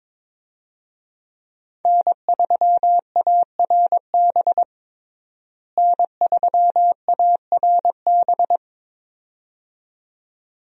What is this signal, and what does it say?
Morse code, heard as a keyed tone, about 22 wpm, 700 Hz. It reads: N3ARB N3ARB
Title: USA Call Signs - 5 Characters - Repeated 2x - 22wpm